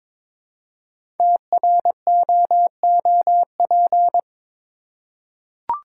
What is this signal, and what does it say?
Morse code, heard as a keyed tone, about 22 wpm, 700 Hz. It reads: TROOP
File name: 5 Letter Words - First 3 Letters Given - 22wpm